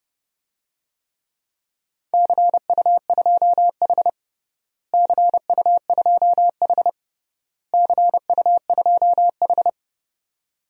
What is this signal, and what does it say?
Morse code, heard as a keyed tone, about 30 wpm, 700 Hz. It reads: CU2H CU2H CU2H
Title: DXCC-100 Call Signs - 4 Characters - Repeated 3x - 30wpm